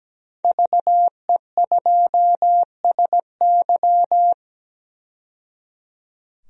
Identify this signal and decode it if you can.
Morse code, heard as a keyed tone, about 17 wpm, 700 Hz. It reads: VE2SY
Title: DXCC-100 Call Signs - 5 Characters - 17wpm